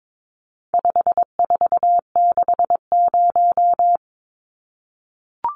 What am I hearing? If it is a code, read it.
Morse code, heard as a keyed tone, about 22 wpm, 700 Hz. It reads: 5460
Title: Numbers - 4-Digit 22wpm